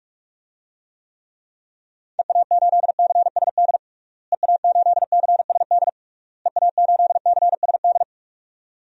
Morse code, heard as a keyed tone, about 42 wpm, 700 Hz. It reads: EA8CSD EA8CSD EA8CSD
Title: DXCC-100 Call Signs - 6 Characters - Repeated 3x - 45wpm